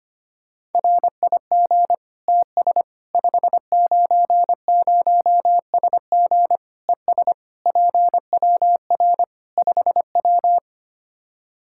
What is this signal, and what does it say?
Morse code, heard as a keyed tone, about 25 wpm, 700 Hz. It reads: RIG TS 590SG ES PWR 5W
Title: Practice QSO - Exchange 2 - Info - 25wpm